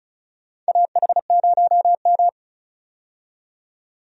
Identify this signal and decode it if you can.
Morse code, heard as a keyed tone, about 36 wpm, 700 Hz. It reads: AH0M
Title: USA Call Signs - 4 Characters - 35wpm